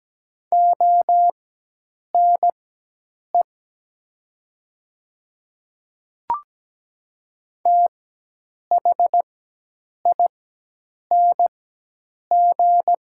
Morse code, heard as a keyed tone, about 17 wpm, spaced as for 8 wpm, 700 Hz. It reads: ONE THING
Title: 4x Character Spacing - Top 100 Words - 17wpm